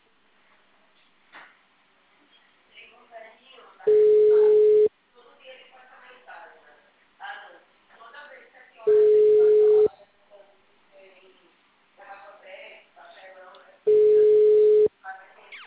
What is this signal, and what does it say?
Ringback tone (Europe, 425 Hz), with speech.